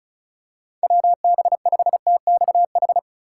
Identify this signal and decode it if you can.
Morse code, heard as a keyed tone, about 36 wpm, 700 Hz. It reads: WB5TXH
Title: USA Call Signs - 2x Character Spacing - 35wpm